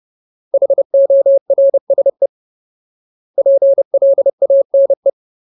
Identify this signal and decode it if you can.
Morse code, heard as a keyed tone, about 30 wpm, 550 Hz. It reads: HORSE PLANE
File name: Ultimate Word List / 30WPM Ultimate